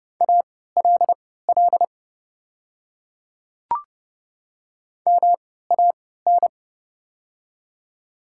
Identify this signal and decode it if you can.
Morse code, heard as a keyed tone, about 30 wpm, spaced as for 17 wpm, 690 Hz. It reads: ALL MAN
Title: 3x Character Spacing - Top 500 Words - 30wpm